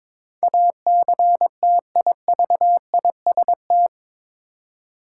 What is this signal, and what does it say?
Morse code, heard as a keyed tone, about 22 wpm, 700 Hz. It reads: ACTIVIST
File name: practice / 22wpm / Eight Letter Words - 22wpm